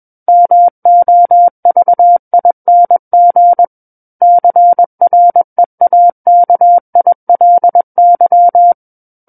Morse code, heard as a keyed tone, about 21 wpm, 700 Hz. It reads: MOVING CREAKILY
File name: Book One / 12-Chapter 12 - 21 WPM